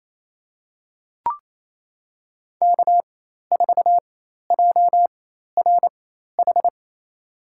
Morse code, heard as a keyed tone, about 28 wpm, spaced as for 13 wpm, 700 Hz. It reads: K4JRH
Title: USA Call Signs - 4x Character Spacing - 28wpm